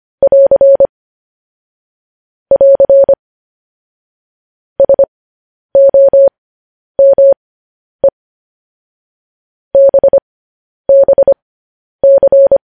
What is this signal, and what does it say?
Morse code, heard as a keyed tone, about 25 wpm, spaced as for 10 wpm, 550 Hz. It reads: <AR> <AR> SOME BBC
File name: podcast / News headlines-1x-25x10